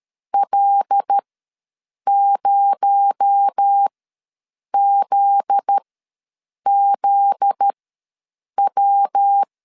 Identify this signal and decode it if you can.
Morse code, heard as a keyed tone, about 13 wpm, spaced as for 7 wpm, 800 Hz. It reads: L0ZZW
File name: LEZIONI / 7 GF9Z / 7 GF9Z ws 12wpm 3321